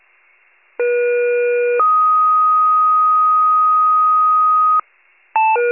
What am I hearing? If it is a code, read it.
Two-tone page, heard as a one-second tone, then a three-second tone, about 480 then 1250 Hz. It is heard from 0.8 to 4.8 s.